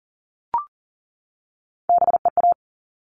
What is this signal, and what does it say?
Morse code, heard as a keyed tone, about 40 wpm, 700 Hz. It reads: BEA
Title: All Cut Numbers - 3-Digit 40wpm